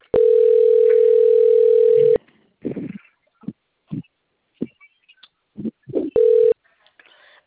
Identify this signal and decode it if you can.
Ringback tone (North America), 440 + 480 Hz.
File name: out- - - - - .27027